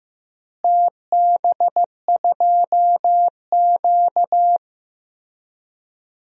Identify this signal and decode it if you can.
Morse code, heard as a keyed tone, about 15 wpm, 700 Hz. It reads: TB2Q
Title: DXCC-100 Call Signs - 4 Characters - Repeated 3x - 15wpm